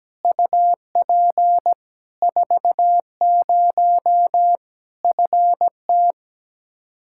Morse code, heard as a keyed tone, about 17 wpm, 700 Hz. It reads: UP 40 FT
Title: Practice QSO - Exchange 2 - Info - 17wpm